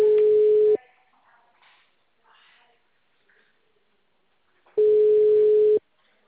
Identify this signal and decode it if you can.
Ringback tone (Europe, 425 Hz).
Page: Piracicaba - SP